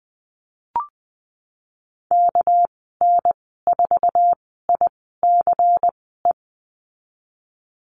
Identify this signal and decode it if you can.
Morse code, heard as a keyed tone, about 20 wpm, spaced as for 14 wpm, 700 Hz. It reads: KN4ICE
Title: 219 Lesson 35g - Number 0 - Call Signs 20wpm